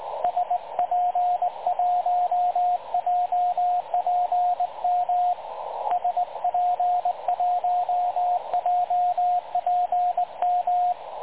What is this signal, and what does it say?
Morse code, heard as a keyed tone, about 20 wpm, 720 Hz, with radio noise. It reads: SP1JPM SP1JPM
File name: dlazaawansowanych / nagraniazpasma / websdr recording 3565.0kHz